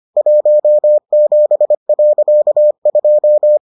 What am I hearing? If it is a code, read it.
Morse code, heard as a keyed tone, about 25 wpm, 600 Hz. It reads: 17.2